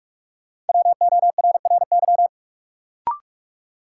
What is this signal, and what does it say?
Morse code, heard as a keyed tone, about 47 wpm, 700 Hz. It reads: WORRY
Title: Sentences from Top 1100 Words - New Word Drill - 45wpm